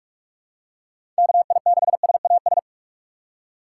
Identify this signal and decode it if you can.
Morse code, heard as a keyed tone, about 43 wpm, 700 Hz. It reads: KI6SAS
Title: USA Call Signs - 6 Characters - 6x Character Spacing - 45wpm